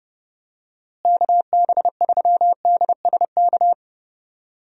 Morse code, heard as a keyed tone, about 32 wpm, 700 Hz. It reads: KB3DSK
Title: USA Call Signs - 6 Characters - Repeated 2x - 30wpm